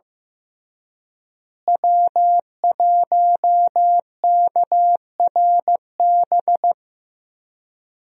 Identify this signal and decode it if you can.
Morse code, heard as a keyed tone, about 15 wpm, 700 Hz. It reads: W1KRB
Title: USA Call Signs - 5 Characters - Repeated 2x - 15wpm